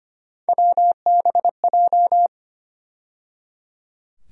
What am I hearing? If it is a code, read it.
Morse code, heard as a keyed tone, about 25 wpm, 700 Hz. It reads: WBJ
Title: USA Call Sign Postfixes - 3 Letters 25wpm